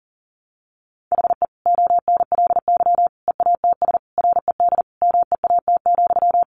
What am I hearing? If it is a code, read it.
Morse code, heard as a keyed tone, about 40 wpm, 700 Hz. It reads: HE ONLY EATS RED MEAT,
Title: Easy Phrases 40wpm